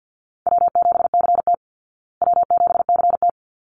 Morse code, heard as a keyed tone, about 50 wpm, 700 Hz. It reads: W7CT W7CT
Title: Easy Call Signs Repeated 2x - 50wpm